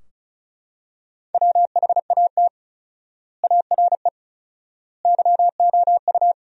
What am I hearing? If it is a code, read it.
Morse code, heard as a keyed tone, about 36 wpm, 700 Hz. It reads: WHAT ARE YOU
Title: 4x Word-Spacing - Sentences from Top 100 Words - 35wpm